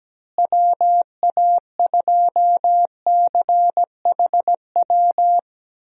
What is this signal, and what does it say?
Morse code, heard as a keyed tone, about 17 wpm, 700 Hz. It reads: WA2CHW
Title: USA Call Signs - 6 Characters - Repeated 3x - 17wpm